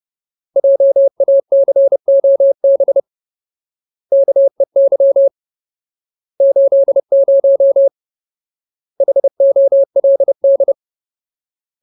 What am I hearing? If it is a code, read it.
Morse code, heard as a keyed tone, about 30 wpm, 550 Hz. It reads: JACOB KEY 80 HOLD
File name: Ultimate Word List / 30WPM Ultimate